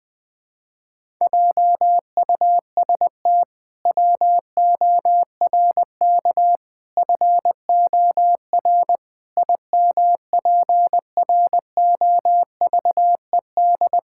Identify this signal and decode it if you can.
Morse code, heard as a keyed tone, about 20 wpm, 700 Hz. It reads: JUST WORK FOR IMPROVED
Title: Pangram Challenge v2 - 20wpm